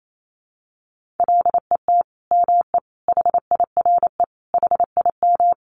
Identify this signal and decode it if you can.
Morse code, heard as a keyed tone, about 28 wpm, 700 Hz. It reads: LET ME HIRE HIM
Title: Sentences from Top 1400 Words 28wpm